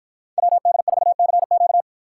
Morse code, heard as a keyed tone, about 48 wpm, 700 Hz. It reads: WD4CQ